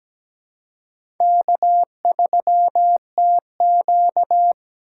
Morse code, heard as a keyed tone, about 17 wpm, 700 Hz. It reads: K3TQ
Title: USA Call Signs - 4 Characters - 4x Character Spacing - 17wpm